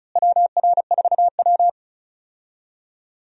Morse code, heard as a keyed tone, about 36 wpm, 700 Hz. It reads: WR4W
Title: USA Call Signs - 4 Characters - Repeated 3x - 35wpm